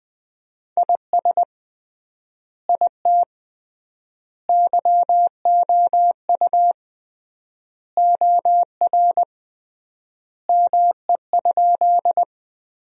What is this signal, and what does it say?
Morse code, heard as a keyed tone, about 20 wpm, 700 Hz. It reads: IS IT YOU OR ME?
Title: 3x Word-Spacing - Sentences from Top 200 Words - 20wpm